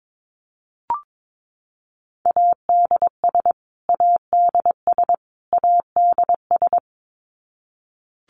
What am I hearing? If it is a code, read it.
Morse code, heard as a keyed tone, about 22 wpm, 700 Hz. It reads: ADS ADS ADS
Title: Abbreviations Repeated 3x - 22wpm